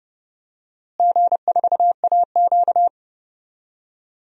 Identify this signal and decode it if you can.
Morse code, heard as a keyed tone, about 30 wpm, 700 Hz. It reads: G4AQ
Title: DXCC-100 Call Signs - 4 Characters - Repeated 2x - 30wpm